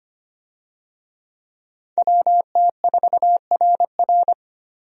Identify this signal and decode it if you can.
Morse code, heard as a keyed tone, about 25 wpm, 700 Hz. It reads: WT4RR
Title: USA Call Signs - 5 Characters - Repeated 2x - 25wpm